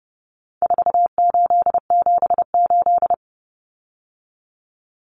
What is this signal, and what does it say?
Morse code, heard as a keyed tone, about 30 wpm, 700 Hz. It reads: 4878
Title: Common Cut Numbers - 4-Digit 30wpm